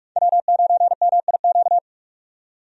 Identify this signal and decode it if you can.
Morse code, heard as a keyed tone, about 42 wpm, 700 Hz. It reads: W9MIQ
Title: USA Call Signs - 5 Characters - 45wpm